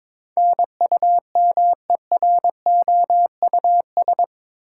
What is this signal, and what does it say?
Morse code, heard as a keyed tone, about 22 wpm, 700 Hz. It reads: NUMEROUS